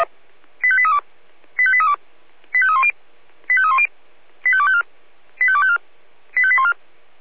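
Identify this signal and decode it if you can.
Five-tone selective call (ZVEI), heard as quick runs of five tones, tones of about 70 ms. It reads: 85832 85832 85310 85310 85254 85254 85814
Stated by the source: Received by SDR as a ZVEI five-tone selective call.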